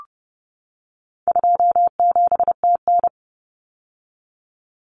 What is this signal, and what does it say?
Morse code, heard as a keyed tone, about 29 wpm, 700 Hz. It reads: 27TN
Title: Common Cut Numbers - 4-Digit 30wpm